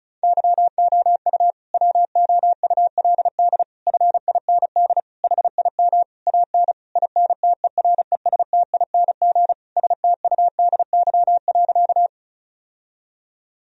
Morse code, heard as a keyed tone, about 36 wpm, 700 Hz. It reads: YOU WOULD FIND HIM AN INTERESTING STUDY.
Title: Sentences from Top 1000 Words - Review Part 1 - 35wpm